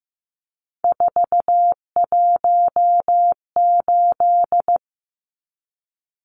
Morse code, heard as a keyed tone, about 15 wpm, 700 Hz. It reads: 418
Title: Numbers - 3-Digit 15wpm